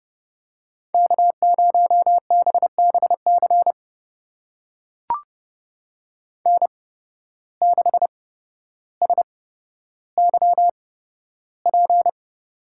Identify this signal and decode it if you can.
Morse code, heard as a keyed tone, about 32 wpm, spaced as for 13 wpm, 700 Hz. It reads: K0BBC N6SYP
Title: USA Call Signs - 5 Characters - 8x Character Spacing - 30wpm